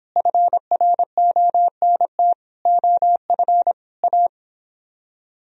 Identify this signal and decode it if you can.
Morse code, heard as a keyed tone, about 29 wpm, 700 Hz. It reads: FRONT OF A